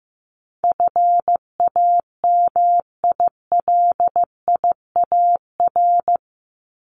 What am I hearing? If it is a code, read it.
Morse code, heard as a keyed tone, about 15 wpm, 700 Hz. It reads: FAMILIAR
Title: Top 751-1000 Words 15wpm